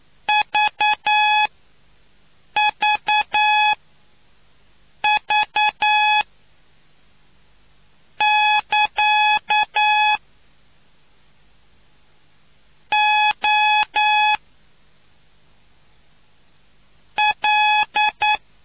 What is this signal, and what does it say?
Morse code, heard as a keyed tone, about 9 wpm, spaced as for 5 wpm, 850 Hz. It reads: VVV<KA> O L